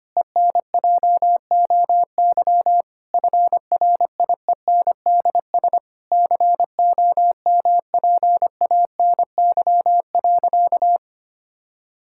Morse code, heard as a keyed tone, about 25 wpm, 700 Hz. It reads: ENJOY FRIENDS COMPANY.
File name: practice / 25wpm / Difficult Sentences with Hints - 25wpm